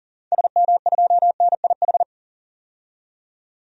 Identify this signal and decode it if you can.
Morse code, heard as a keyed tone, about 37 wpm, 700 Hz. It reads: SM2NIH
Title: DXCC-100 Call Signs - 6 Characters - 40wpm